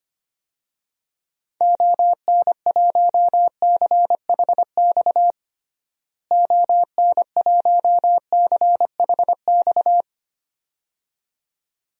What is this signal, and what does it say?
Morse code, heard as a keyed tone, about 25 wpm, 700 Hz. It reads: ON1CHX ON1CHX
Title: DXCC-100 Call Signs - 6 Characters - Repeated 2x - 25wpm